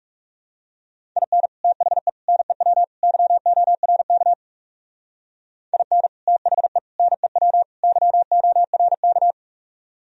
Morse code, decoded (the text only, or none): IN THE NEW YORK IN THE NEW YORK